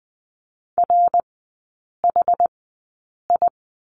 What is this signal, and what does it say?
Morse code, heard as a keyed tone, about 20 wpm, spaced as for 8 wpm, 700 Hz. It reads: RHI
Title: 074 Lesson 13e - Letter L - Three Characters 20wpm